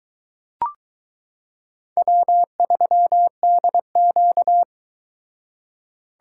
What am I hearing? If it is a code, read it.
Morse code, heard as a keyed tone, about 22 wpm, 700 Hz. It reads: W3DQ